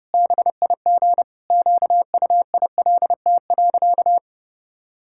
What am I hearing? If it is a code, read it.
Morse code, heard as a keyed tone, about 30 wpm, 700 Hz. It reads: BIG QUILT.
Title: Pangram Challenge v2 - 30wpm